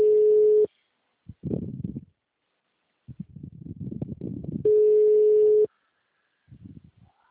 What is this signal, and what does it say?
Ringback tone (Europe, 425 Hz).